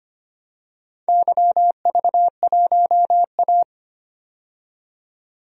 Morse code, heard as a keyed tone, about 25 wpm, 700 Hz. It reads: YV1A
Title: DXCC-100 Call Signs - 4 Characters - 25wpm